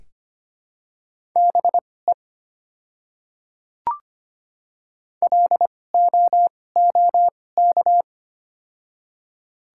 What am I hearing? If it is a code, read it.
Morse code, heard as a keyed tone, about 25 wpm, spaced as for 18 wpm, 700 Hz. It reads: BE LOOK
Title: 2x Character Spacing - Top 100 Words - 25wpm